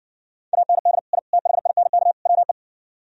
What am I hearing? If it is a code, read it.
Morse code, heard as a keyed tone, about 59 wpm, 700 Hz. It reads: AND I THINK WE